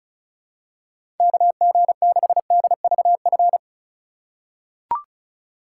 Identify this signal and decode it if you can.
Morse code, heard as a keyed tone, about 34 wpm, 700 Hz. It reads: KG6DVF
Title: USA Call Signs - 3x Character Spacing - 35wpm